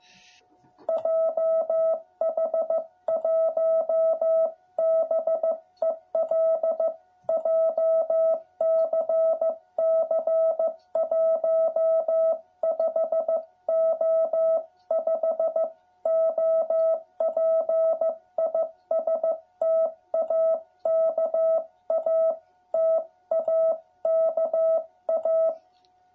Morse code, heard as a keyed tone, about 15 wpm, 660 Hz. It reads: JH1BEL JCC 15O5 OP IS TAKA TAKA
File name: jh8cbh / 2021cw